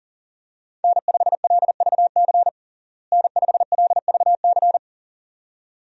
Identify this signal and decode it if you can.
Morse code, heard as a keyed tone, about 40 wpm, 700 Hz. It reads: N5LVC N5LVC
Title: USA Call Signs - 5 Characters - Repeated 2x - 40wpm